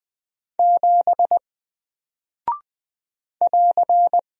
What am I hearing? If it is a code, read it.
Morse code, heard as a keyed tone, about 20 wpm, 700 Hz. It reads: 7 <AR>
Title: Single Character (Letter:Number:Punctuation:Prosign) Alt 20wpm